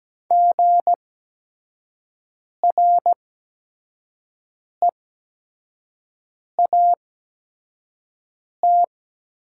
Morse code, heard as a keyed tone, about 17 wpm, spaced as for 5 wpm, 700 Hz. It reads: GREAT
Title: 8x Character Spacing - Top 300 Words - 17wpm